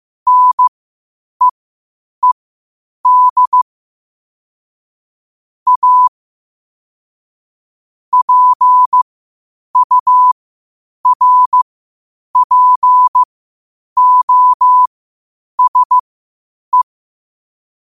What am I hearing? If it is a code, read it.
Morse code, heard as a keyed tone, about 15 wpm, spaced as for 8 wpm, 1000 Hz. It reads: NEED A PURPOSE
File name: qotdfiles / 8wpm